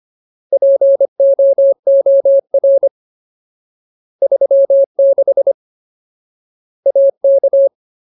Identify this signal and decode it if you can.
Morse code, heard as a keyed tone, about 25 wpm, 550 Hz. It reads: POOR 36 AK
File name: Ultimate Word List / 25WPM Ultimate